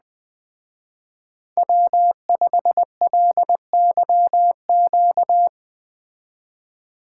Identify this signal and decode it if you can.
Morse code, heard as a keyed tone, about 20 wpm, 700 Hz. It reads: W5LYQ